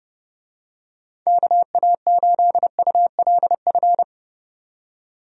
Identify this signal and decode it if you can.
Morse code, heard as a keyed tone, about 29 wpm, 700 Hz. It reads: KA8ULF
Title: USA Call Signs - 6 Characters - 30wpm